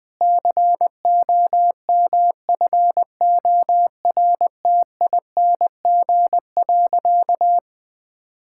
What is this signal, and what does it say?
Morse code, heard as a keyed tone, about 20 wpm, 700 Hz. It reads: COMFORTING.
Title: Difficult Sentences with Hints - 20wpm